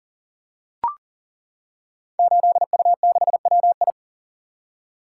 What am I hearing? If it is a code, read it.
Morse code, heard as a keyed tone, about 42 wpm, 690 Hz. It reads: 8U6WI
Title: Code Groups - 5 Letter:Numbers - 40wpm